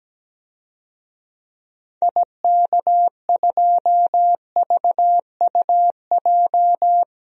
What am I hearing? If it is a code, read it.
Morse code, heard as a keyed tone, about 17 wpm, 700 Hz. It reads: IK2VUJ
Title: DXCC-100 Call Signs - 6 Characters - Repeated 3x - 17wpm